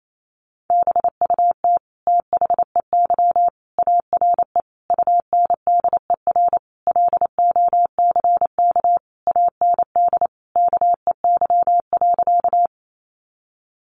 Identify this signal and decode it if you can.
Morse code, heard as a keyed tone, about 28 wpm, 700 Hz. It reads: BUT THEY ARE UNDER LOCK AND KEY.
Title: Sentences from Top 1300 Words 28wpm